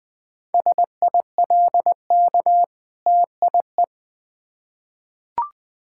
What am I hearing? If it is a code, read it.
Morse code, heard as a keyed tone, about 20 wpm, 700 Hz. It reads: SILK TIE
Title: Difficult Words with Hints - 20wpm